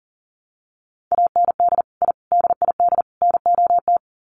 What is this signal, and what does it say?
Morse code, heard as a keyed tone, about 40 wpm, 700 Hz. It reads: AND I DID NOT
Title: Sets of 4 Words 40wpm